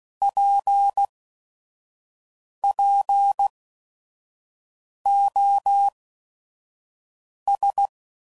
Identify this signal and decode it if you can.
Morse code, heard as a keyed tone, about 16 wpm, 800 Hz. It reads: P P O S